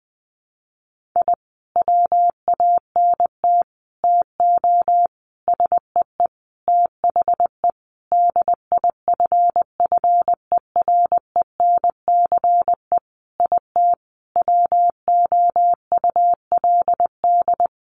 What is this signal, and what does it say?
Morse code, heard as a keyed tone, about 20 wpm, 700 Hz. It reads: I WANT TO SEE THE DIFFERENCE IT WOULD
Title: Sentences from Top 1300 Words 20wpm